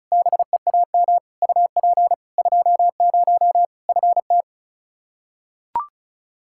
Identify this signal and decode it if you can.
Morse code, heard as a keyed tone, about 36 wpm, 700 Hz. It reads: BEAM UP 20 FT
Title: K1USN SST QSO Practice 35wpm